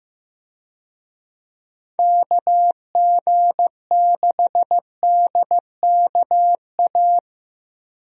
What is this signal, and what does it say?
Morse code, heard as a keyed tone, about 16 wpm, 700 Hz. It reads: KG6DKA
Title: USA Call Signs - 6 Characters - 15wpm